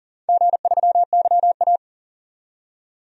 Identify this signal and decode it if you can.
Morse code, heard as a keyed tone, about 40 wpm, 700 Hz. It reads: G3YA
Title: DXCC-100 Call Signs - 4 Characters - 40wpm